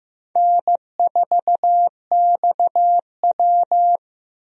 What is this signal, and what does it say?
Morse code, heard as a keyed tone, about 15 wpm, 700 Hz. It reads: N4XW
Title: USA Call Signs - 4 Characters - Repeated 2x - 15wpm